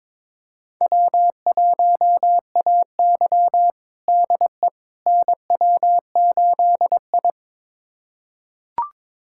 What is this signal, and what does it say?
Morse code, heard as a keyed tone, about 22 wpm, 700 Hz. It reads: W1AY DE NW8I
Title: USA Callsign DE Callsign - 4 Characters - 22wpm